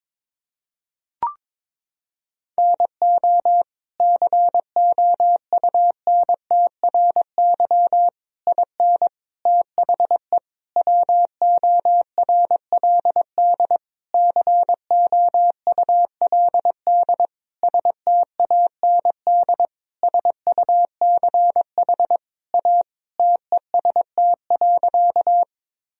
Morse code, heard as a keyed tone, about 22 wpm, 700 Hz. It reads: NO COUNTRY IN THE WORLD COULD STAND SUCH A TEST.
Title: Sentences from Top 1000 Words - Review Part 3 - 22wpm